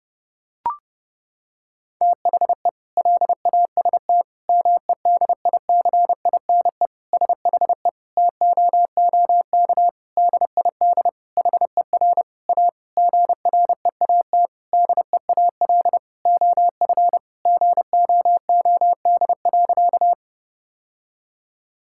Morse code, heard as a keyed tone, about 30 wpm, 700 Hz. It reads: THE LAST MEDICINE SHE TOOK DID HER A GREAT DEAL OF GOOD.
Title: Sentences from Top 1000 Words - Review Part 6 - 30wpm